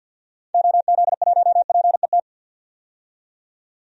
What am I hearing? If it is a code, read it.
Morse code, heard as a keyed tone, about 46 wpm, 700 Hz. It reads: OZ1PET
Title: DXCC-100 Call Signs - 6 Characters - 50wpm